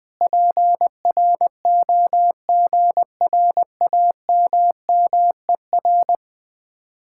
Morse code, heard as a keyed tone, about 20 wpm, 700 Hz. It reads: PROGRAMMER
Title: Speed-Racing Common QSO Phrases Encore - 20wpm